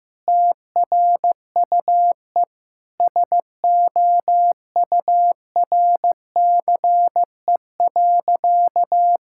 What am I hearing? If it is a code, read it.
Morse code, heard as a keyed tone, about 15 wpm, 700 Hz. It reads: TRUE SOURCE.
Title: Sentences from Top 1000 Words - Review Part 5 - 15wpm